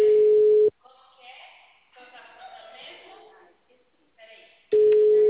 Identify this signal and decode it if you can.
Ringback tone (Europe, 425 Hz), with speech.